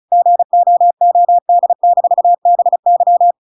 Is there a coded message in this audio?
Yes: Morse code from 0.1 to 3.3 s, heard as a keyed tone, about 34 wpm, 700 Hz. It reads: GOOD-BY